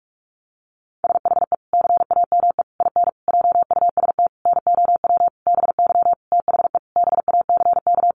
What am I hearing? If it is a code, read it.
Morse code, heard as a keyed tone, about 44 wpm, 700 Hz. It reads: SHE CAME IN JUST NOW BY THE BACK
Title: Sentences from Top 300 Words 45wpm